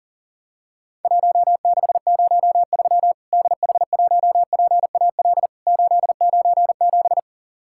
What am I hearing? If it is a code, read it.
Morse code, heard as a keyed tone, about 40 wpm, 700 Hz. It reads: 1603 DH1PAL 897
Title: Worked All Europe DX Contest - QTC Exchange - 40wpm